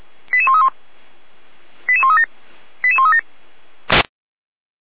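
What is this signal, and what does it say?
Five-tone selective call (ZVEI), heard as quick runs of five tones, tones of about 70 ms. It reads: 88131 88137 88137
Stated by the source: ZVEI five-tone call 88130 received by SDR.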